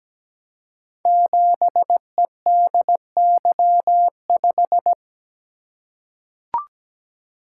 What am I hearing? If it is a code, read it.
Morse code, heard as a keyed tone, about 17 wpm, 700 Hz. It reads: 7EDY5
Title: Code Groups - 5 Letter:Numbers - 17wpm